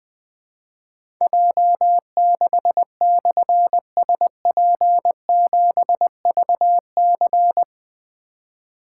Morse code, heard as a keyed tone, about 21 wpm, 700 Hz. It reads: J6/SP7VC
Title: Difficult Call Signs 20wpm